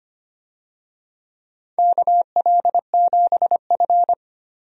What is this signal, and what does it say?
Morse code, heard as a keyed tone, about 25 wpm, 700 Hz. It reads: KL7F
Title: DXCC-100 Call Signs - 4 Characters - Repeated 3x - 25wpm